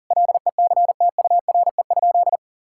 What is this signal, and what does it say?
Morse code, heard as a keyed tone, about 40 wpm, 700 Hz. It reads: LECTURE?
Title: Sentences from Top 1600 Words 40wpm